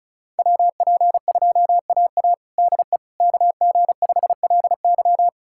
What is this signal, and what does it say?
Morse code, heard as a keyed tone, about 34 wpm, 700 Hz. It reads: WP2AA DE KG5LY
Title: USA Callsign DE Callsign - 5 Characters - 35wpm